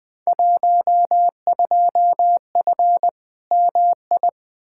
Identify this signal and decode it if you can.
Morse code, heard as a keyed tone, about 20 wpm, 700 Hz. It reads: 12F MI
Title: Field Day Exchanges Encore - 20wpm